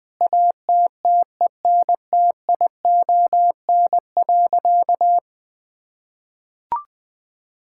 Morse code, heard as a keyed tone, about 20 wpm, 700 Hz. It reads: ATTENTION.
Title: Sentences from Top 1000 Words - Review Part 6 - 20wpm